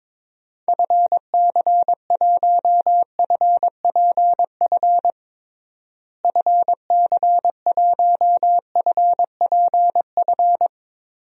Morse code, heard as a keyed tone, about 21 wpm, 700 Hz. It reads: FC1FPF FC1FPF
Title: DXCC-100 Call Signs - 6 Characters - Repeated 3x - 22wpm